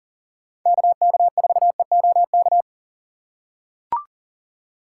Morse code, heard as a keyed tone, about 37 wpm, 700 Hz. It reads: KK4EOK
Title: USA Call Signs - 6 Characters - 4x Character Spacing - 40wpm